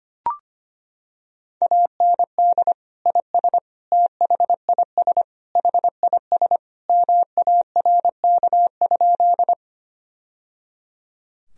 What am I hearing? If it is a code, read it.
Morse code, heard as a keyed tone, about 25 wpm, 700 Hz. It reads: AND IS THIS HIS MARK?
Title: Sentences from Top 1000 Words - Review Part 1 - 25wpm